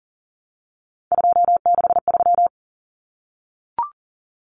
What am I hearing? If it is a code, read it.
Morse code, heard as a keyed tone, about 40 wpm, 700 Hz. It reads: 263
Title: Numbers - 3-Digit 40wpm